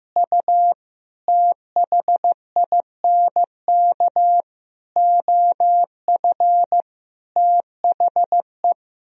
Morse code, heard as a keyed tone, about 15 wpm, 700 Hz. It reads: U THINK OF THE
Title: Sentences from Top 300 Words and Abbreviations 15wpm